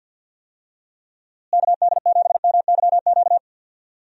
Morse code, heard as a keyed tone, about 50 wpm, 700 Hz. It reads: KD7MYQ
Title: USA Call Signs - 6 Characters - Repeated 3x - 50wpm